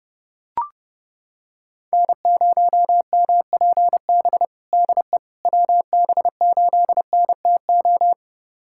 Morse code, heard as a keyed tone, about 30 wpm, 700 Hz. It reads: N0MPB DE WB8NTO
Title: USA Callsign DE Callsign 30wpm